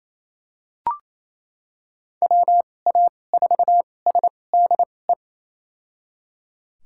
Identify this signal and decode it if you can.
Morse code, heard as a keyed tone, about 28 wpm, spaced as for 20 wpm, 700 Hz. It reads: WA4SDE
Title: USA Call Signs - 6 Characters - 2x Character Spacing - 28wpm